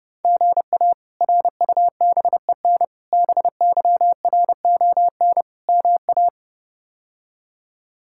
Morse code, heard as a keyed tone, about 30 wpm, 700 Hz. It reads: GA RUBEN BYRON MA
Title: K1USN SST QSO Practice - 30wpm